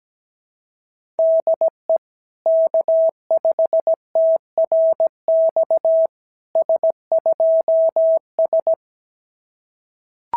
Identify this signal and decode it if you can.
Morse code, heard as a keyed tone, about 17 wpm, 650 Hz. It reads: DE K5TRX S2S
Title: SOTA QSO Practice 17wpm